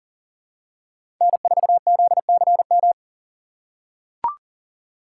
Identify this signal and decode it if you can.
Morse code, heard as a keyed tone, about 40 wpm, 700 Hz. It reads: N4ZCM
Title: USA Call Signs - 5 Characters - 8x Character Spacing - 40wpm